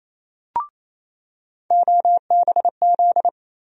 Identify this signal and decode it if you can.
Morse code, heard as a keyed tone, about 28 wpm, 700 Hz. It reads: OBZ
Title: USA Call Sign Postfixes - 3 Letters 28wpm